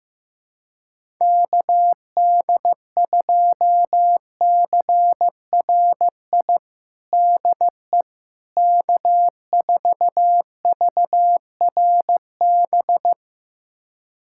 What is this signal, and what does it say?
Morse code, heard as a keyed tone, about 15 wpm, 700 Hz. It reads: KD2CRI DE K4VRB
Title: USA Callsign DE Callsign 15wpm